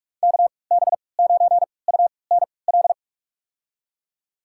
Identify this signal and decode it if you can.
Morse code, heard as a keyed tone, about 44 wpm, spaced as for 25 wpm, 700 Hz. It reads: KB9UNL
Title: USA Call Signs - 6 Characters - 3x Character Spacing - 45wpm